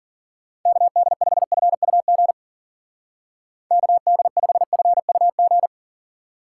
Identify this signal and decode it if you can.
Morse code, heard as a keyed tone, about 50 wpm, 700 Hz. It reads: KD5FUG KD5FUG
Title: Speed Racing - USA Call Signs 40wpm